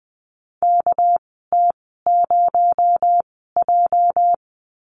Morse code, heard as a keyed tone, about 20 wpm, spaced as for 14 wpm, 700 Hz. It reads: KT0J
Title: 231 Lesson 37g - Letter x - Call Signs 20wpm